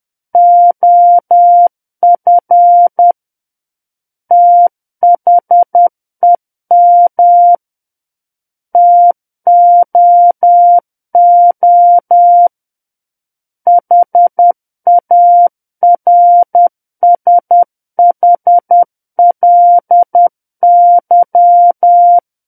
Morse code, heard as a keyed tone, about 10 wpm, 700 Hz. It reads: OF THEM TOO HARSHLY
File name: Book One / 01-Chapter 1 - 10 WPM